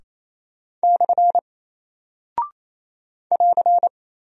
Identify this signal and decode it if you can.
Morse code, heard as a keyed tone, about 28 wpm, 700 Hz. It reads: / <AR>
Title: Single Character (Letter:Number:Punctuation:Prosign) 28wpm